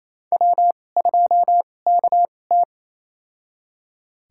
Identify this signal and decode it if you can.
Morse code, heard as a keyed tone, about 27 wpm, spaced as for 20 wpm, 700 Hz. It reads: W2KT
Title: USA Call Signs - 4 Characters - 2x Character Spacing - 28wpm